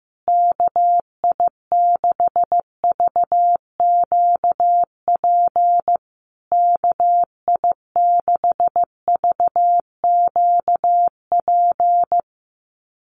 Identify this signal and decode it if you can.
Morse code, heard as a keyed tone, about 15 wpm, 700 Hz. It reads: KI6VQP KI6VQP
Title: USA Call Signs Repeated 3x 15wpm